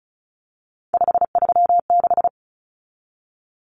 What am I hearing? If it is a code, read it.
Morse code, heard as a keyed tone, about 34 wpm, 700 Hz. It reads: 536
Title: Common Cut Numbers - 3-Digit 35wpm